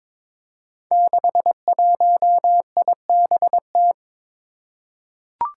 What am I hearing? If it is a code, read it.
Morse code, heard as a keyed tone, about 22 wpm, 700 Hz. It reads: 61IBT
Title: Five Challenging Letter-Number Combinations 22wpm